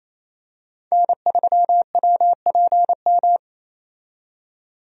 Morse code, heard as a keyed tone, about 27 wpm, 700 Hz. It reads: N3WPM
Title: USA Call Signs - 5 Characters - 28wpm